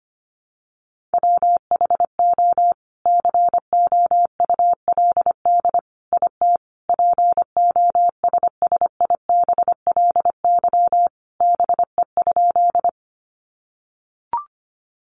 Morse code, heard as a keyed tone, about 25 wpm, 700 Hz. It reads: WHO COULD IT POSSIBLY BE?
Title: Sentences from Top 1300 Words 25wpm